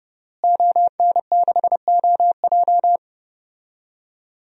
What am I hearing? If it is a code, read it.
Morse code, heard as a keyed tone, about 32 wpm, 700 Hz. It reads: ON6OJ
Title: DXCC-100 Call Signs - 5 Characters - Repeated 3x - 30wpm